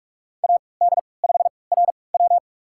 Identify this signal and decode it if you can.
Morse code, heard as a keyed tone, about 46 wpm, spaced as for 25 wpm, 700 Hz. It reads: AD5RW
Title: USA Call Signs - 5 Characters - 3x Character Spacing - 45wpm